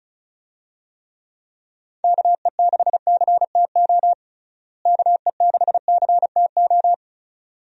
Morse code, heard as a keyed tone, about 35 wpm, 700 Hz. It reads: KE6CTO KE6CTO
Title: USA Call Signs - 6 Characters - Repeated 2x - 35wpm